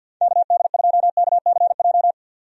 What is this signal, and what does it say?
Morse code, heard as a keyed tone, about 52 wpm, 700 Hz. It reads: KD2KCJ
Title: USA Call Signs - 6 Characters - 50wpm